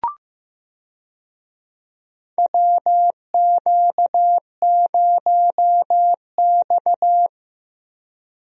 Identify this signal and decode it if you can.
Morse code, heard as a keyed tone, about 15 wpm, 700 Hz. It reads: WQ0X
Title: USA Call Signs - 4 Characters - Repeated 3x - 15wpm